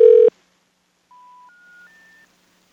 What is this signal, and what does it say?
Special information tone.